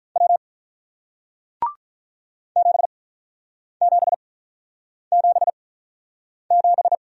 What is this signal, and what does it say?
Morse code, heard as a keyed tone, about 48 wpm, spaced as for 9 wpm, 700 Hz. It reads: W 7777